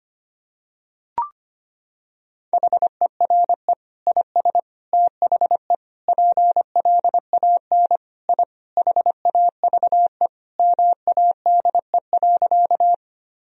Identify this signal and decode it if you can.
Morse code, heard as a keyed tone, about 25 wpm, 700 Hz. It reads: HERE IS THE PLAN I HAVE MADE.
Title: Sentences from Top 1000 Words - Review Part 2 - 25wpm